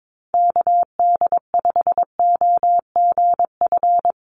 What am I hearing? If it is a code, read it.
Morse code, heard as a keyed tone, about 22 wpm, 700 Hz. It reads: KD5OGF
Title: USA Call Signs Repeated 2x 22wpm